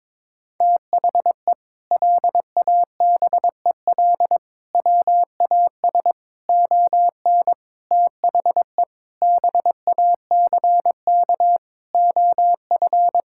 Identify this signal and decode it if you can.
Morse code, heard as a keyed tone, about 22 wpm, 700 Hz. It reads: THE LABEL WAS ON THE BACK OF
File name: practice / 22wpm / Sentences with 4-Word Phrases - 22wpm